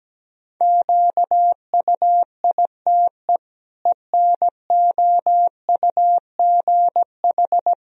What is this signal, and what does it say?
Morse code, heard as a keyed tone, about 17 wpm, 700 Hz. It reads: QUITE ENOUGH
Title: Sentences from Top 1000 Words - Review Part 1 - 17wpm